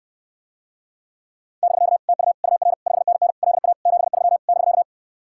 Morse code, heard as a keyed tone, about 69 wpm, 700 Hz. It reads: <BK> TU UR 5NN LA 73 <BK>